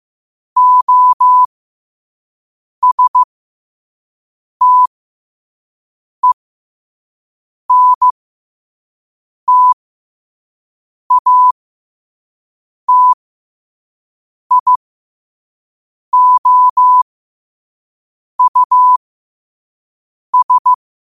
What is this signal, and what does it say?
Morse code, heard as a keyed tone, about 15 wpm, spaced as for 5 wpm, 1000 Hz. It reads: OSTENTATIOUS
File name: qotdfiles / 5wpm